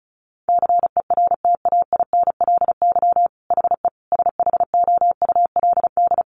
Morse code, heard as a keyed tone, about 34 wpm, 700 Hz. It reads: CERTAINLY HE SHOULD
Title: Sentences from Top 900 Words 35wpm